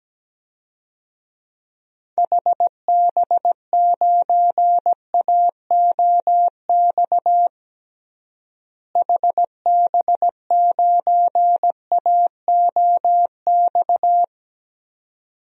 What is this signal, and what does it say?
Morse code, heard as a keyed tone, about 17 wpm, 700 Hz. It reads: HB9AOX HB9AOX
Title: DXCC-100 Call Signs - 6 Characters - Repeated 2x - 17wpm